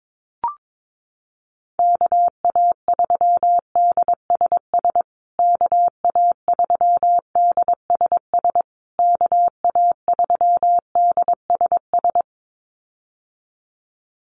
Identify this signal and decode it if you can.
Morse code, heard as a keyed tone, about 22 wpm, 690 Hz. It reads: KA3DSS KA3DSS KA3DSS
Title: USA Call Signs Repeated 3x 22wpm